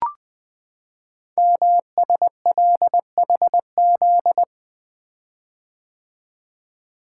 Morse code, heard as a keyed tone, about 20 wpm, 700 Hz. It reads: MSLHZ
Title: Code Groups - 5 Letter:Numbers - 20wpm